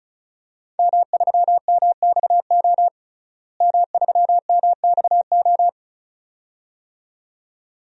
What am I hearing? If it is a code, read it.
Morse code, heard as a keyed tone, about 34 wpm, 700 Hz. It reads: M3MXO M3MXO
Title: DXCC-100 Call Signs - 5 Characters - Repeated 2x - 35wpm